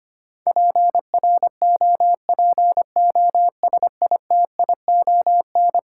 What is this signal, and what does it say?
Morse code, heard as a keyed tone, about 25 wpm, 700 Hz. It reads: PROPOSITION
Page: morse-code-ninja-practice-25wpm